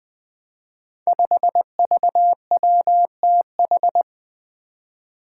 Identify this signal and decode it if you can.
Morse code, heard as a keyed tone, about 20 wpm, 700 Hz. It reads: 5VWTH
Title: Five Challenging Letter-Number Combinations 20wpm